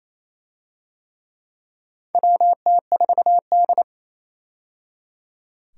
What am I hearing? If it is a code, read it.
Morse code, heard as a keyed tone, about 28 wpm, 700 Hz. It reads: WT4D
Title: USA Call Signs - 4 Characters - 28wpm